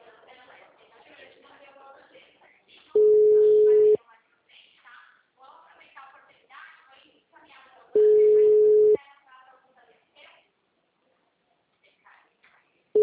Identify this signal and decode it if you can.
Ringback tone (Europe, 425 Hz), with speech.
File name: out- -2002- - - .1323